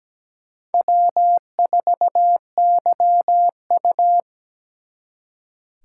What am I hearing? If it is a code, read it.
Morse code, heard as a keyed tone, about 20 wpm, 700 Hz. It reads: W4YU
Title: Speed Racing - USA Call Signs 17wpm